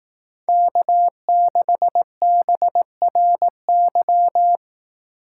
Morse code, heard as a keyed tone, about 18 wpm, 700 Hz. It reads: K6BRY
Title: Speed Racing - USA 1x3 Call Signs 15wpm